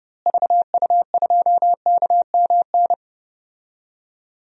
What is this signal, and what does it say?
Morse code, heard as a keyed tone, about 32 wpm, 700 Hz. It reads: VU2KMN
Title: DXCC-100 Call Signs - 6 Characters - Repeated 2x - 30wpm